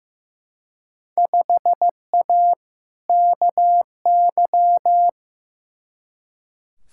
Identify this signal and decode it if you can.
Morse code, heard as a keyed tone, about 15 wpm, 700 Hz. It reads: 5A KY
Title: Field Day Exchanges - 15wpm